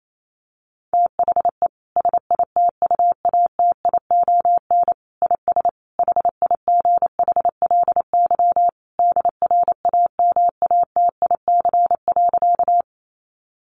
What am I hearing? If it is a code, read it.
Morse code, heard as a keyed tone, about 28 wpm, 700 Hz. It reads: THE SITUATION IS HIGHLY DRAMATIC.
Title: Sentences from Top 1300 Words 28wpm